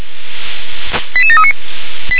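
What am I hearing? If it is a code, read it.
Five-tone selective call (ZVEI), heard as quick runs of five tones, tones of about 70 ms. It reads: 88739
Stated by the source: ZVEI five-tone call 88737 received by SDR.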